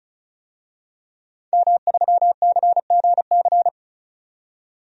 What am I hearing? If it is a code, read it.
Morse code, heard as a keyed tone, about 35 wpm, 700 Hz. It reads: M3CGC
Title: DXCC-100 Call Signs - 5 Characters - 35wpm